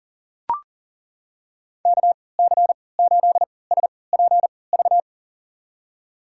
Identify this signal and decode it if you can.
Morse code, heard as a keyed tone, about 40 wpm, spaced as for 23 wpm, 700 Hz. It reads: KC8SPV
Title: USA Call Signs - 6 Characters - 3x Character Spacing - 40wpm